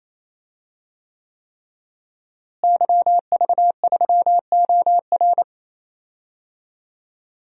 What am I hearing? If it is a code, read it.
Morse code, heard as a keyed tone, about 28 wpm, 700 Hz. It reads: YV3OR
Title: DXCC-100 Call Signs - 5 Characters - Repeated 3x - 28wpm